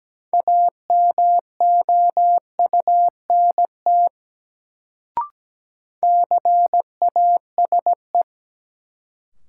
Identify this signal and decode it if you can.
Morse code, heard as a keyed tone, about 17 wpm, 700 Hz. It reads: AMOUNT CASE
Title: Sentences from Top 700 Words - New Word Drill - 17wpm